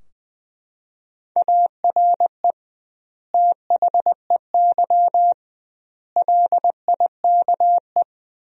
Morse code, heard as a keyed tone, about 20 wpm, 700 Hz. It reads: ARE THEY LIKE
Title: 2x Word-Spacing - Sentences from Top 200 Words - 20wpm